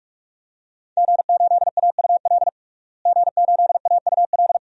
Morse code, heard as a keyed tone, about 41 wpm, 700 Hz. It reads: G8AUL G8AUL
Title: DXCC-100 Call Signs - 5 Characters - Repeated 2x - 45wpm